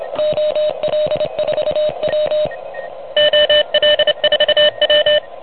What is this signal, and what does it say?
Morse code, heard as a keyed tone, about 31 wpm, 590 Hz. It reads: OL4W OL4W
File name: truSDX / MWC29.8